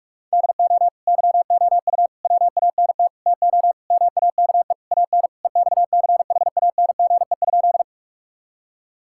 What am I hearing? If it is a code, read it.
Morse code, heard as a keyed tone, about 46 wpm, 700 Hz. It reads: DO YOU WANT TO MAKE AN EXCHANGE?
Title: Sentences from Top 1000 Words - Review Part 6 - 45wpm